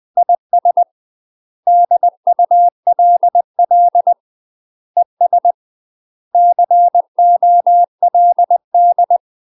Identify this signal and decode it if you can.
Morse code, heard as a keyed tone, about 20 wpm, 700 Hz. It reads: IS DULL ES COLD
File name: UK MASTER QSOs TRAINING / 20WPM / UK MASTER QSOs 0004